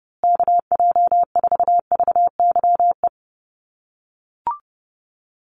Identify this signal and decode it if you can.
Morse code, heard as a keyed tone, about 30 wpm, 700 Hz. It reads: KJ4VYE
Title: USA Call Signs Repeated 2x 30wpm